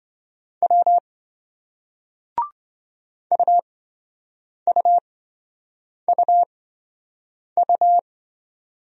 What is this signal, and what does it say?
Morse code, heard as a keyed tone, about 26 wpm, spaced as for 7 wpm, 700 Hz. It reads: W UUUU